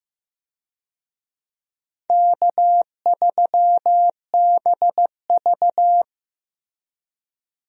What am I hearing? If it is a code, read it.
Morse code, heard as a keyed tone, about 16 wpm, 700 Hz. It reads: K3BV
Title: USA Call Signs - 4 Characters - Repeated 3x - 15wpm